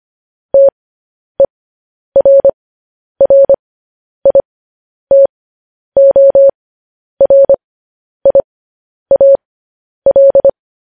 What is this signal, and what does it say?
Morse code, heard as a keyed tone, about 25 wpm, spaced as for 10 wpm, 550 Hz. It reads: TERRITORIAL
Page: news headlines in morse code at 10 wpm